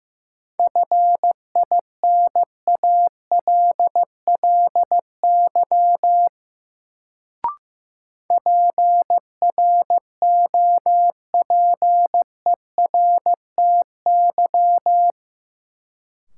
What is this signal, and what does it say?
Morse code, heard as a keyed tone, about 15 wpm, 700 Hz. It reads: FINALLY PROPERTY
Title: Sentences from Top 1000 Words - New Word Drill - 15wpm